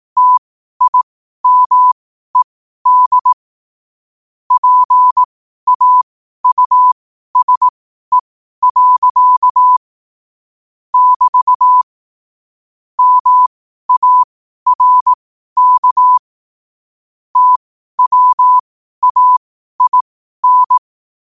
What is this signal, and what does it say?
Morse code, heard as a keyed tone, about 18 wpm, spaced as for 12 wpm, 1000 Hz. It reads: TIMED PAUSE. <BT> MARK TWAIN
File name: qotdfiles / 12wpm